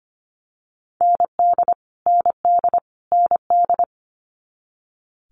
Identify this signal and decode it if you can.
Morse code, heard as a keyed tone, about 25 wpm, 700 Hz. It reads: ND ND ND
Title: Multi-letter Phonemes Repeated 3x 25wpm